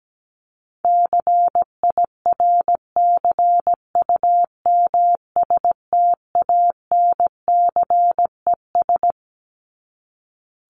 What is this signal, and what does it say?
Morse code, heard as a keyed tone, about 17 wpm, 700 Hz. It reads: CIRCUMSTANCES
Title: ES:S Suffix - Top 100 Words with Base Words 17wpm